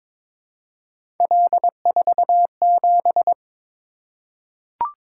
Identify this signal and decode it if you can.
Morse code, heard as a keyed tone, about 23 wpm, 700 Hz. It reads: L47
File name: practice / 22wpm / Three Letter-Number Combinations - 22wpm